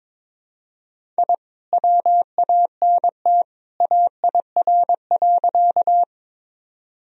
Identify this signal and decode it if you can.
Morse code, heard as a keyed tone, about 22 wpm, 700 Hz. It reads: I WANT AIR.
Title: Sentences from Top 1000 Words - Review Part 1 - 22wpm